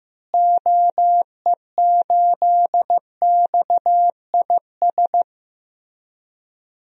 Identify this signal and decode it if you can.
Morse code, heard as a keyed tone, about 15 wpm, 700 Hz. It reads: OE8XIS
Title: DXCC-100 Call Signs - 6 Characters - Repeated 3x - 15wpm